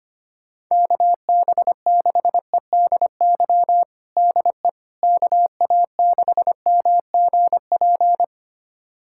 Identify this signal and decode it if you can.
Morse code, heard as a keyed tone, about 25 wpm, 700 Hz. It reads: KB6EDY DE KA6MGP
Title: USA Callsign DE Callsign - 6 Characters - 25wpm